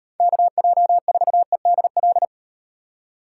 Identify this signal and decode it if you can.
Morse code, heard as a keyed tone, about 38 wpm, 700 Hz. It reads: KJ4EDL